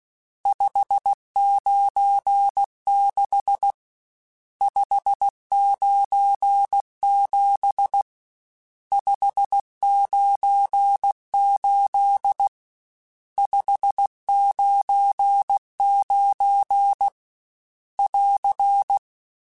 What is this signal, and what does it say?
Morse code, heard as a keyed tone, about 16 wpm, 800 Hz. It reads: 596 597 598 599 <AR>
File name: RSTs / RST-mp3s / Various RST @15